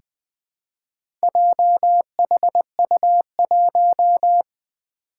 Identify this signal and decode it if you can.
Morse code, heard as a keyed tone, about 21 wpm, 700 Hz. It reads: JHU1
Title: Four Challenging Letter-Number Combinations 20wpm